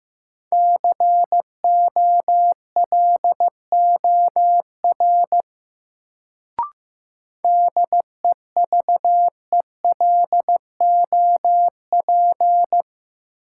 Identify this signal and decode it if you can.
Morse code, heard as a keyed tone, about 15 wpm, 700 Hz. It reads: COLOR DEVELOP
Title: Sentences from Top 1000 Words - Review Part 2 - New Word Drill - 15wpm